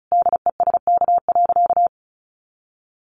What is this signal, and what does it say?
Morse code, heard as a keyed tone, about 34 wpm, 700 Hz. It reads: DESK.
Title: Additional Easy Phrases 35wpm